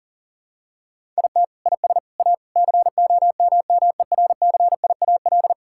Morse code, heard as a keyed tone, about 40 wpm, 700 Hz. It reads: IT IS A COMMERCIAL
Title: Sentences from Top 1000 Words - Review Part 6 - 40wpm